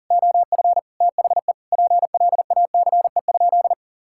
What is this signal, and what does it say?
Morse code, heard as a keyed tone, about 40 wpm, 700 Hz. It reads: OF THE PLACE?
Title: Sentences from Top 1600 Words 40wpm